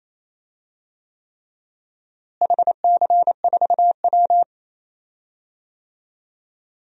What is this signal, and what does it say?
Morse code, heard as a keyed tone, about 28 wpm, 700 Hz. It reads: HC4W
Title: DXCC-100 Call Signs - 4 Characters - Repeated 3x - 28wpm